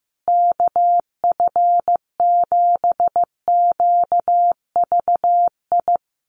Morse code, heard as a keyed tone, about 15 wpm, 700 Hz. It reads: KF7QVI
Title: USA Call Signs Repeated 3x 15wpm